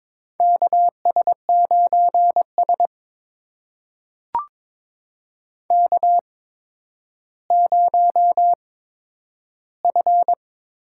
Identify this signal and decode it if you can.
Morse code, heard as a keyed tone, about 22 wpm, spaced as for 11 wpm, 700 Hz. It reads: KS9S K0F
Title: USA Call Signs - 4 Characters - 8x Character Spacing - 22wpm